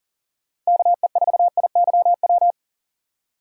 Morse code, heard as a keyed tone, about 40 wpm, 700 Hz. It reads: KE4IYW
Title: USA Call Signs - 6 Characters - 2x Character Spacing - 40wpm